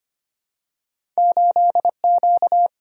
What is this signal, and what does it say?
Morse code, heard as a keyed tone, about 25 wpm, 700 Hz. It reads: 8Q
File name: Two Letter-Number Combinations - 25wpm